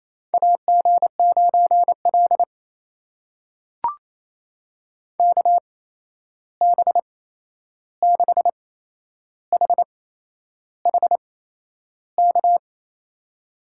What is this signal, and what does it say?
Morse code, heard as a keyed tone, about 28 wpm, spaced as for 11 wpm, 700 Hz. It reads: AG9L KB6HHK
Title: USA Call Signs - 8x Character Spacing - 28wpm